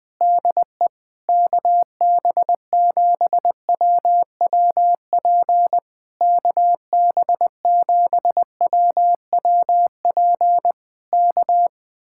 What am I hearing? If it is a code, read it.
Morse code, heard as a keyed tone, about 20 wpm, 700 Hz. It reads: DE KB7WWP KB7WWP K